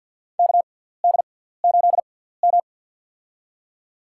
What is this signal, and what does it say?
Morse code, heard as a keyed tone, about 53 wpm, spaced as for 17 wpm, 700 Hz. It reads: KD8M
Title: USA Call Signs - 4 Characters - 6x Character Spacing - 50wpm